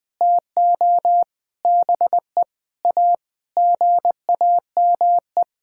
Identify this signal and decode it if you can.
Morse code, heard as a keyed tone, about 20 wpm, 700 Hz. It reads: TO BE A GAME
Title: Sentences with 6-Word Phrases - 20wpm